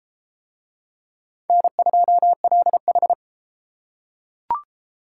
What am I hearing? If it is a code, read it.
Morse code, heard as a keyed tone, about 32 wpm, 700 Hz. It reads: N2LH